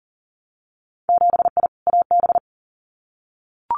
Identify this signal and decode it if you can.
Morse code, heard as a keyed tone, about 39 wpm, 700 Hz. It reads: 7I AB
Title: Winter Field Day Exchanges Encore 40wpm